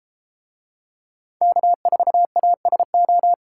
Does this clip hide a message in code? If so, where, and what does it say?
Yes: Morse code from 1.4 to 3.3 s, heard as a keyed tone, about 32 wpm, 700 Hz. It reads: K4ASO